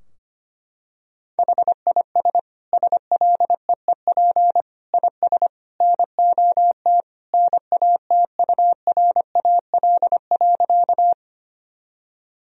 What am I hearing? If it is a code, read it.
Morse code, heard as a keyed tone, about 25 wpm, 700 Hz. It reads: HIS SLEEP IS NOT NATURAL.
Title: Sentences from Top 1000 Words - Review Part 4 - 25wpm